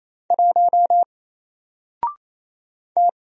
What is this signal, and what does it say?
Morse code, heard as a keyed tone, about 27 wpm, 700 Hz. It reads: 1 T
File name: Single-Letter-Number-28wpm